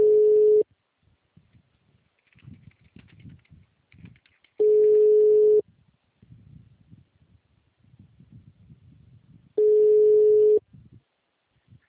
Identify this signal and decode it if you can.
Ringback tone (Europe, 425 Hz).
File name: out- -2006- - - .162